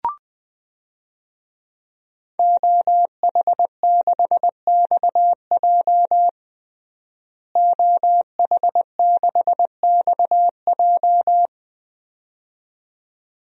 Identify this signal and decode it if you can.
Morse code, heard as a keyed tone, about 21 wpm, 700 Hz. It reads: OH6XJ OH6XJ
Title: DXCC-100 Call Signs - 5 Characters - Repeated 2x - 20wpm